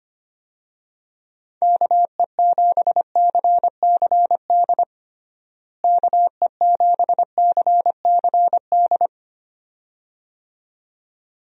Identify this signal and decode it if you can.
Morse code, heard as a keyed tone, about 25 wpm, 700 Hz. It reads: KE7CCD KE7CCD
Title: USA Call Signs - 6 Characters - Repeated 2x - 25wpm